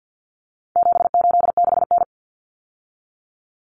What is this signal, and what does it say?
Morse code, heard as a keyed tone, about 50 wpm, 700 Hz. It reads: 786N
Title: Common Cut Numbers - 4-Digit 50wpm